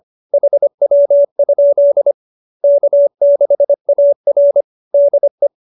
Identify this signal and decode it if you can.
Morse code, heard as a keyed tone, about 25 wpm, 590 Hz. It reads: HW? K6AR DE